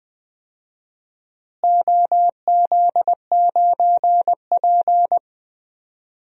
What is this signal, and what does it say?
Morse code, heard as a keyed tone, about 20 wpm, 700 Hz. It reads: OZ9P
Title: DXCC-100 Call Signs - 4 Characters - Repeated 3x - 20wpm